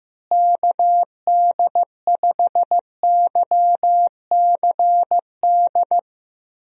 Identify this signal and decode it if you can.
Morse code, heard as a keyed tone, about 16 wpm, 700 Hz. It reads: KD5YCD
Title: USA Call Signs - 6 Characters - Repeated 2x - 15wpm